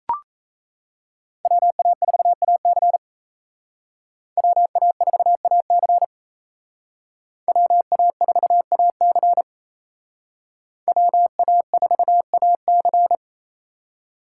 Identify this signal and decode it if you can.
Morse code, heard as a keyed tone, about 36 wpm, 700 Hz. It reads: WA4AC WA4AC WA4AC WA4AC